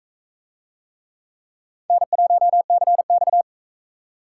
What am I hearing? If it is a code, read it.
Morse code, heard as a keyed tone, about 38 wpm, 700 Hz. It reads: N1CX